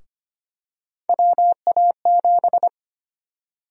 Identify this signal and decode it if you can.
Morse code, heard as a keyed tone, about 25 wpm, 700 Hz. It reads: WA7
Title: USA Call Sign Prefixes - Repeated 2x - 25wpm